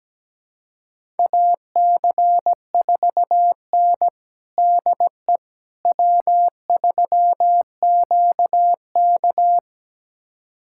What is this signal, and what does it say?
Morse code, heard as a keyed tone, about 17 wpm, 700 Hz. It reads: AC4N DE W3QK
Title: USA Callsign DE Callsign - 4 Characters - 17wpm